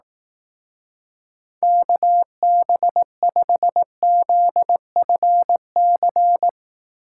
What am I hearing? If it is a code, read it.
Morse code, heard as a keyed tone, about 19 wpm, 700 Hz. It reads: KB5ZFC
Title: Speed Racing - USA 2x3 Call Signs 15wpm